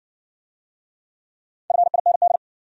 Morse code, heard as a keyed tone, about 55 wpm, 700 Hz. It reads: FIND